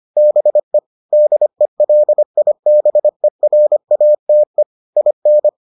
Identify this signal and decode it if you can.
Morse code, heard as a keyed tone, about 25 wpm, 600 Hz. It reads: BE DELIBERATE IN